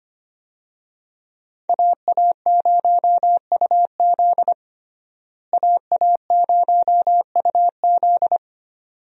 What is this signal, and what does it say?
Morse code, heard as a keyed tone, about 25 wpm, 700 Hz. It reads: AA0UZ AA0UZ
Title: USA Call Signs - 5 Characters - Repeated 2x - 25wpm